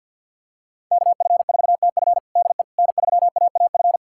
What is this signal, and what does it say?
Morse code, heard as a keyed tone, about 51 wpm, 700 Hz. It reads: KF4TF DE N3AAF
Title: USA Callsign DE Callsign - 5 Characters - 50wpm